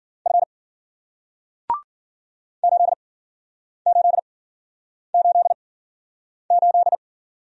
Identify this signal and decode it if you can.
Morse code, heard as a keyed tone, about 52 wpm, 700 Hz. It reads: F 8 8 8 8